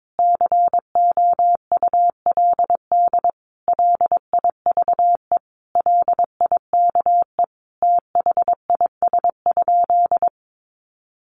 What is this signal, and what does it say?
Morse code, heard as a keyed tone, about 22 wpm, 700 Hz. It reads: COULD LIVE LIKE THIS?
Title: Sentences from Top 1400 Words 22wpm